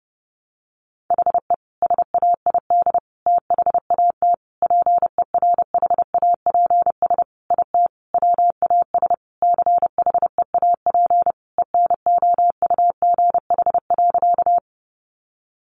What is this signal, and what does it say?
Morse code, heard as a keyed tone, about 30 wpm, 700 Hz. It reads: HE SAID THAT PERHAPS IT WAS CHEAP ENOUGH.
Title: Sentences from Top 1200 Words 30wpm